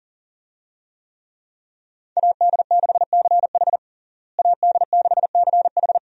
Morse code, heard as a keyed tone, about 40 wpm, 700 Hz. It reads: AD6CH AD6CH
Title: USA Call Signs - 5 Characters - Repeated 2x - 40wpm